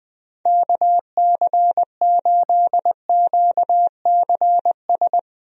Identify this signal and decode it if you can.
Morse code, heard as a keyed tone, about 20 wpm, 700 Hz. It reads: KC8QCS
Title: USA Call Signs - 6 Characters - Repeated 3x - 20wpm